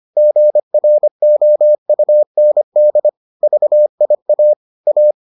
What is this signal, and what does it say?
Morse code, heard as a keyed tone, about 25 wpm, 600 Hz. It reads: GROUND VIA A